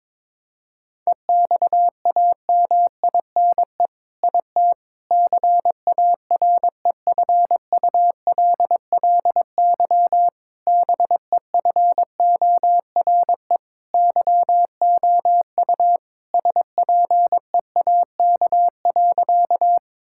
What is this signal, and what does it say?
Morse code, heard as a keyed tone, about 21 wpm, 700 Hz. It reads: EXAMINE IT CAREFULLY BEFORE YOU SPEAK.
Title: Sentences from Top 1600 Words 22wpm